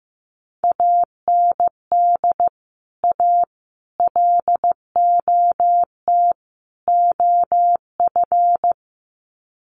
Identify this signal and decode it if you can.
Morse code, heard as a keyed tone, about 15 wpm, 700 Hz. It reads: AND A LOT OF
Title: Sets of 4 Words 15wpm